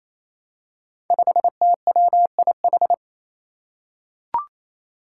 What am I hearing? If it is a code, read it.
Morse code, heard as a keyed tone, about 28 wpm, 700 Hz. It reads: 5TWIH
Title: Five Challenging Letter-Number Combinations 28wpm